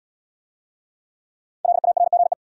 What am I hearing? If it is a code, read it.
Morse code, heard as a keyed tone, about 80 wpm, 700 Hz. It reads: LARGE